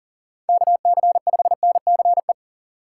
Morse code, heard as a keyed tone, about 40 wpm, 700 Hz. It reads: KC5NCE
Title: USA Call Signs - 6 Characters - 2x Character Spacing - 40wpm